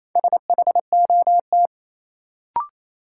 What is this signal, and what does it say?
Morse code, heard as a keyed tone, about 28 wpm, 700 Hz. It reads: SHOT
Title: Sentences from Top 1100 Words - New Word Drill Alt - 28wpm